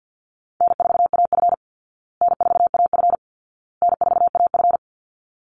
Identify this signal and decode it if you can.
Morse code, heard as a keyed tone, about 49 wpm, 700 Hz. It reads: N4AF N4AF N4AF
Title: Call Signs Repeated 2x - 50wpm